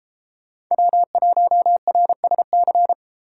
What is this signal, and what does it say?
Morse code, heard as a keyed tone, about 32 wpm, 700 Hz. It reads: W1RSC